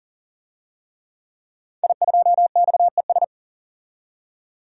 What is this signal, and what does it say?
Morse code, heard as a keyed tone, about 40 wpm, 700 Hz. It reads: I2XES
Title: DXCC-100 Call Signs - 5 Characters - 40wpm